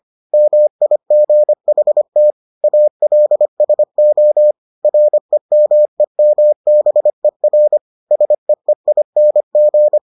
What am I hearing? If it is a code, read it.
Morse code, heard as a keyed tone, about 25 wpm, 600 Hz. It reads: MIGHT ALSO REMEMBER SEEING